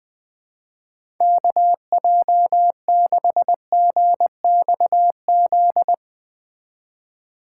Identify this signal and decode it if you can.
Morse code, heard as a keyed tone, about 20 wpm, 700 Hz. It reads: KJ6GXZ
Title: USA Call Signs - 6 Characters - 2x Character Spacing - 20wpm